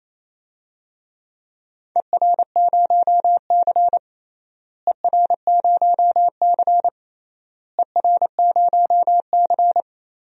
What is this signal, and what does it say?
Morse code, heard as a keyed tone, about 28 wpm, 700 Hz. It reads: ER0C ER0C ER0C
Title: DXCC-100 Call Signs - 4 Characters - Repeated 3x - 28wpm